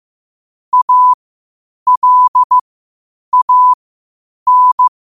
Morse code, heard as a keyed tone, about 15 wpm, spaced as for 8 wpm, 1000 Hz. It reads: ALAN
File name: qotdfiles / 8wpm